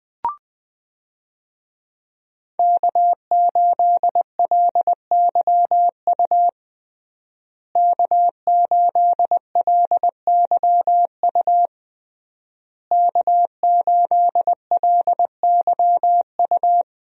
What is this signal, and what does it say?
Morse code, heard as a keyed tone, about 20 wpm, 700 Hz. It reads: K8LYU K8LYU K8LYU
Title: USA Call Signs - 5 Characters - Repeated 3x - 20wpm